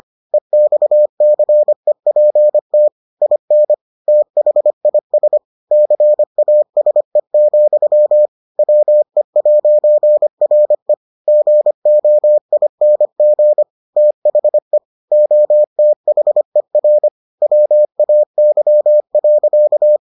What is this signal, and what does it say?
Morse code, heard as a keyed tone, about 25 wpm, 600 Hz. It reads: EXCEPT IN THIS CASE, WE'RE GOING THE OTHER WAY.